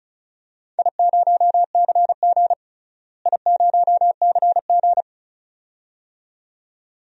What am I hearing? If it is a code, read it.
Morse code, heard as a keyed tone, about 36 wpm, 700 Hz. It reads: I0CG I0CG
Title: DXCC-100 Call Signs - 4 Characters - Repeated 2x - 35wpm